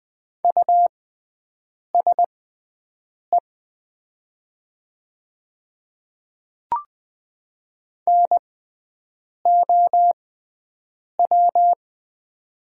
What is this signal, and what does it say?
Morse code, heard as a keyed tone, about 20 wpm, spaced as for 7 wpm, 690 Hz. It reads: USE NOW
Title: 6x Character Spacing - Top 100 Words - 20wpm